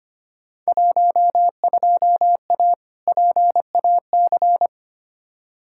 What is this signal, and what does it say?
Morse code, heard as a keyed tone, about 25 wpm, 700 Hz. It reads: 12A PAC
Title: Field Day Exchanges Encore - 25wpm